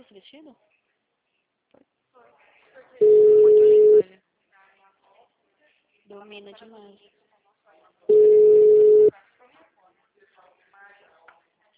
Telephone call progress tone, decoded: ringback tone

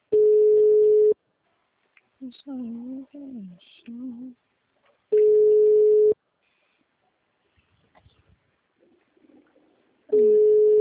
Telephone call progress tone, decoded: ringback tone

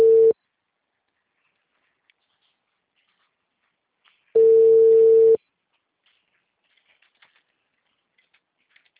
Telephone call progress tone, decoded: ringback tone

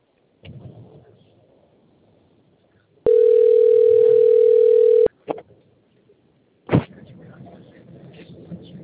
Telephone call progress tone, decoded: ringback tone

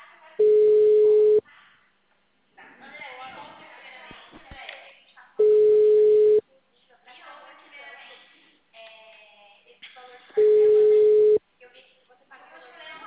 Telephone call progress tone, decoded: ringback tone

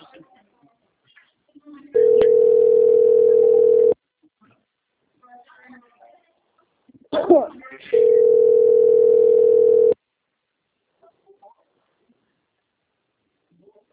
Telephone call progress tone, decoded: ringback tone

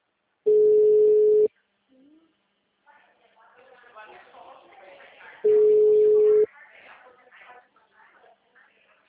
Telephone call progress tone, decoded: ringback tone

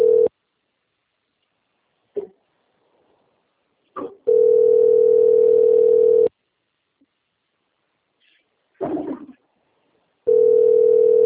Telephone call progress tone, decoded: ringback tone

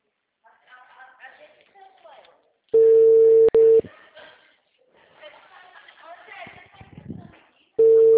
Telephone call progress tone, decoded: ringback tone

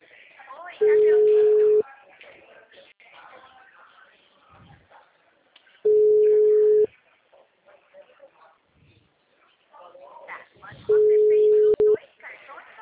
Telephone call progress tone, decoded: ringback tone

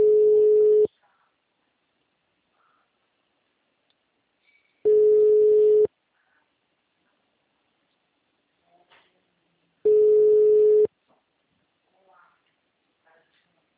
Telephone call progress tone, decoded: ringback tone

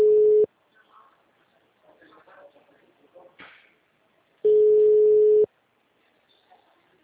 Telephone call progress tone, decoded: ringback tone

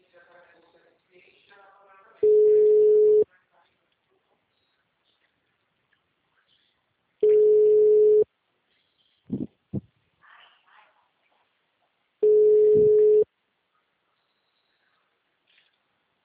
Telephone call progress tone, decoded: ringback tone

